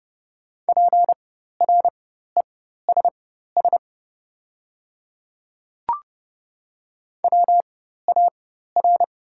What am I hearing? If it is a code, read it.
Morse code, heard as a keyed tone, about 30 wpm, spaced as for 14 wpm, 700 Hz. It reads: PRESS WAR
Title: 4x Character Spacing - Top 500 Words - 30wpm